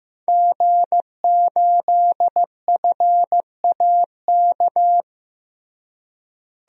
Morse code, heard as a keyed tone, about 15 wpm, 700 Hz. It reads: G8FAK
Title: DXCC-100 Call Signs - 5 Characters - Repeated 2x - 15wpm